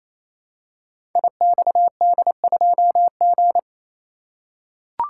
Morse code, heard as a keyed tone, about 28 wpm, 700 Hz. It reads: IXD2G
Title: Code Groups - 5 Letter:Numbers - 28wpm